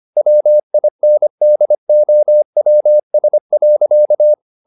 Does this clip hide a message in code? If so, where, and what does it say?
Yes: Morse code from 0.2 to 4.3 s, heard as a keyed tone, about 25 wpm, 600 Hz. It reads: WINDOWS.